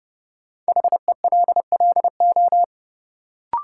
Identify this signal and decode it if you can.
Morse code, heard as a keyed tone, about 30 wpm, 700 Hz. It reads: HELLO
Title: Sentences from Top 1100 Words - New Word Drill Alt - 30wpm